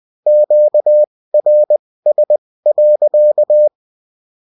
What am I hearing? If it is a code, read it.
Morse code, heard as a keyed tone, about 20 wpm, spaced as for 16 wpm, 600 Hz. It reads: QRS.